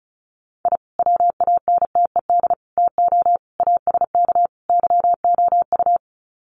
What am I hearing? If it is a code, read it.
Morse code, heard as a keyed tone, about 34 wpm, 700 Hz. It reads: I WANTED TO ASK YOU
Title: Sets of 5 Words Encore2 35wpm